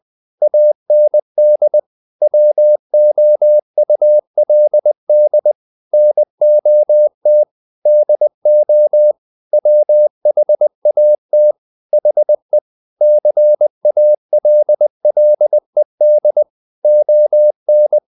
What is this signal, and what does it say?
Morse code, heard as a keyed tone, about 20 wpm, 600 Hz. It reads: AND WOULD NOT DO WHAT HE CALLED ON
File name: the old man and the sea part 2 20wpm